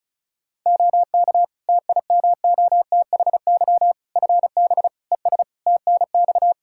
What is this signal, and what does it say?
Morse code, heard as a keyed tone, about 36 wpm, 700 Hz. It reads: OK TIMOTHY FB ES TNX
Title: Practice QSO - Exchange 2 - Info - 35wpm